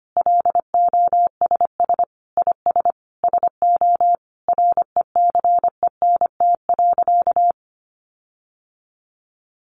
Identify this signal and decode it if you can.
Morse code, heard as a keyed tone, about 25 wpm, 700 Hz. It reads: LOSS IS SO RECENT.
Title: Sentences from Top 1400 Words 25wpm